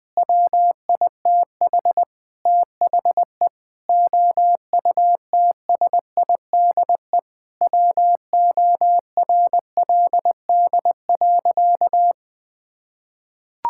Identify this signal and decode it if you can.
Morse code, heard as a keyed tone, about 20 wpm, 700 Hz. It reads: WITH THE OUTSIDE WORLD.
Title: Sentences from Top 1000 Words - Review Part 6 - 20wpm